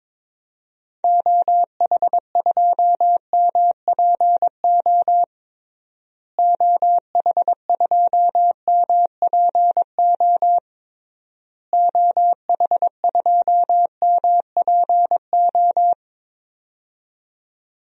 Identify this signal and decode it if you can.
Morse code, heard as a keyed tone, about 22 wpm, 700 Hz. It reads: OH2MPO OH2MPO OH2MPO
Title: DXCC-100 Call Signs - 6 Characters - Repeated 3x - 22wpm